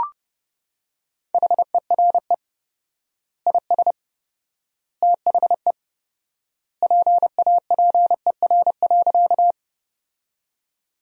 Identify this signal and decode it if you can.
Morse code, heard as a keyed tone, about 32 wpm, 700 Hz. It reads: HERE IS THE PAPER.
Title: 4x Word-Spacing - Sentences from Top 300 Words - 30wpm